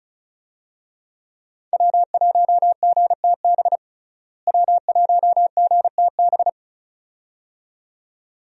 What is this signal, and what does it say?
Morse code, heard as a keyed tone, about 34 wpm, 700 Hz. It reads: W1GTB W1GTB
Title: USA Call Signs - 5 Characters - Repeated 2x - 35wpm